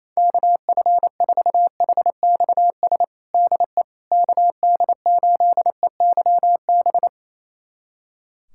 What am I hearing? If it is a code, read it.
Morse code, heard as a keyed tone, about 28 wpm, 700 Hz. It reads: KF4HXS DE KD8EYB
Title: USA Callsign DE Callsign - 6 Characters - 28wpm